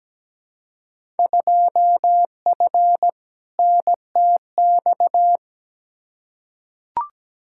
Morse code, heard as a keyed tone, about 17 wpm, 700 Hz. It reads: 2F NTX
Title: Field Day Exchanges - 17wpm